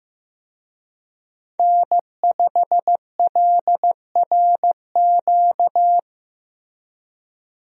Morse code, heard as a keyed tone, about 15 wpm, 700 Hz. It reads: N5LRQ
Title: USA Call Signs - 5 Characters - Repeated 2x - 15wpm